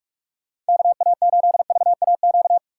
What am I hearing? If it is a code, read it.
Morse code, heard as a keyed tone, about 59 wpm, 700 Hz. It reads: KA8VAQ